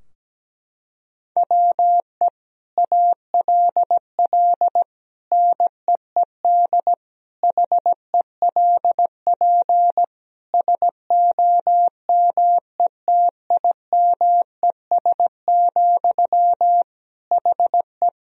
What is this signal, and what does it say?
Morse code, heard as a keyed tone, about 17 wpm, 700 Hz. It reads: WE ALL NEED HELP SOMETIMES, HE
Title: Sentences from Top 1000 Words - Review Part 3 - 17wpm